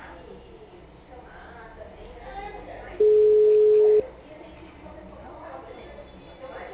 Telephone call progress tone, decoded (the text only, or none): ringback tone